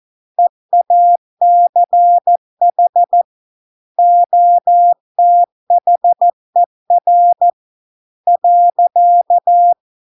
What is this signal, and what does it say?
Morse code, heard as a keyed tone, about 14 wpm, 700 Hz. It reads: EACH OTHER .